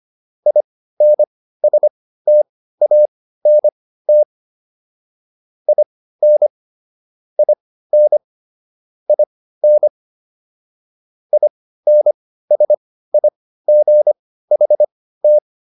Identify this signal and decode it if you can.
Morse code, heard as a keyed tone, about 25 wpm, spaced as for 15 wpm, 600 Hz. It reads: INSTANT IN IN IN INSIGHT